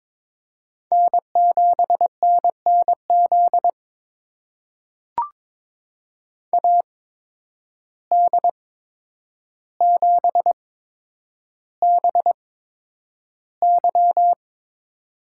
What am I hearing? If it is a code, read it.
Morse code, heard as a keyed tone, about 22 wpm, spaced as for 9 wpm, 700 Hz. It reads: N7NNZ AD7BY
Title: USA Call Signs - 5 Characters - 8x Character Spacing - 22wpm